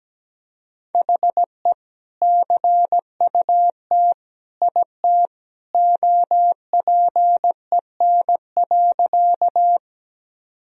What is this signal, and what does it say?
Morse code, heard as a keyed tone, about 17 wpm, 700 Hz. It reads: HE CUT IT OPEN.
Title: Sentences from Top 1000 Words - Review Part 1 - 17wpm